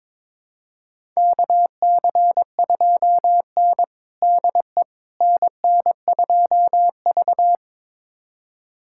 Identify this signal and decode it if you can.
Morse code, heard as a keyed tone, about 22 wpm, 700 Hz. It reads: KC2N DE NN2V
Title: USA Callsign DE Callsign - 4 Characters - 22wpm